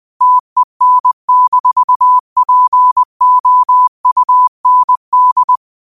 Morse code, heard as a keyed tone, about 20 wpm, 1000 Hz. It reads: TEN-POUND
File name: qotdfiles / 20wpm